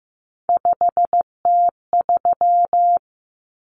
Morse code, heard as a keyed tone, about 15 wpm, 700 Hz. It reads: 5T3
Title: Common Cut Numbers - 3-Digit 15wpm